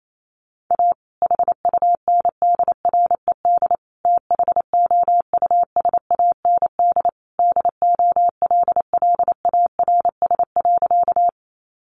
Morse code, heard as a keyed tone, about 28 wpm, 700 Hz. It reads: A HUNDRED THOUSAND DOLLARS.
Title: A Sound Of Thunder 28wpm